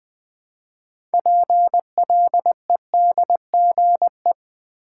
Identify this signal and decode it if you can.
Morse code, heard as a keyed tone, about 20 wpm, 700 Hz. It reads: PLEDGE
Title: 6 Letter Words - First 3 Letters Given - 20wpm